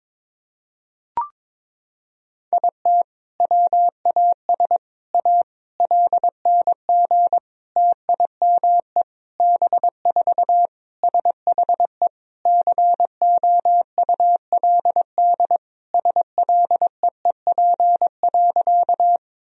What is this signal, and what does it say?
Morse code, heard as a keyed tone, about 22 wpm, 700 Hz. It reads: IT WAS A LNG TIME B4 SHE COULD SLEEP.
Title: Sentences from Top 500 Words and Abbreviations 22wpm